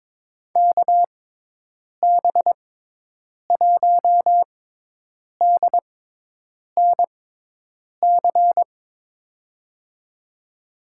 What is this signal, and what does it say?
Morse code, heard as a keyed tone, about 22 wpm, spaced as for 8 wpm, 700 Hz. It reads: KB1DNC
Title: USA Call Signs - 6x Character Spacing - 22wpm